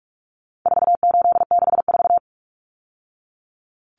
Morse code, heard as a keyed tone, about 44 wpm, 700 Hz. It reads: 4864
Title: Numbers - 4-Digit 45wpm